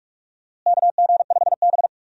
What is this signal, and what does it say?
Morse code, heard as a keyed tone, about 42 wpm, 700 Hz. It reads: KG5B
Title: USA Call Signs - 4 Characters - Repeated 2x - 45wpm